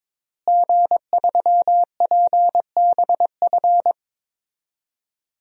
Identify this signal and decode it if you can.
Morse code, heard as a keyed tone, about 22 wpm, 700 Hz. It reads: G3PBF
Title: DXCC-100 Call Signs - 5 Characters - Repeated 3x - 22wpm